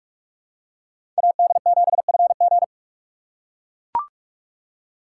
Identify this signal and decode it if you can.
Morse code, heard as a keyed tone, about 43 wpm, 700 Hz. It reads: AD7FG
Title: USA Call Signs - 2x Character Spacing - 45wpm